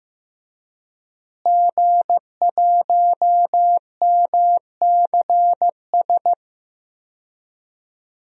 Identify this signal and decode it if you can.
Morse code, heard as a keyed tone, about 15 wpm, 700 Hz. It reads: G1MCS
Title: DXCC-100 Call Signs - 5 Characters - Repeated 3x - 15wpm